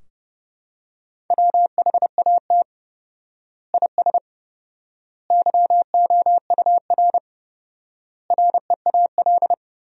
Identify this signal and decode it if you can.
Morse code, heard as a keyed tone, about 30 wpm, 700 Hz. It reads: WHAT IS YOUR REAL
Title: 4x Word-Spacing - Sentences from Top 300 Words - 30wpm